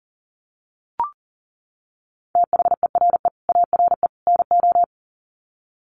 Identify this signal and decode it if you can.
Morse code, heard as a keyed tone, about 40 wpm, 700 Hz. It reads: THERE ARE NO
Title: Sets of 3 Words 40wpm